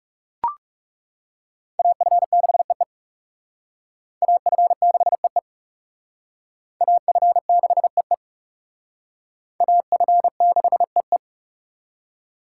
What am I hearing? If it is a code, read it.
Morse code, heard as a keyed tone, about 37 wpm, 700 Hz. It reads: AF6EE AF6EE AF6EE AF6EE
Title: Speed Racing - USA 2x2 Call Signs 30wpm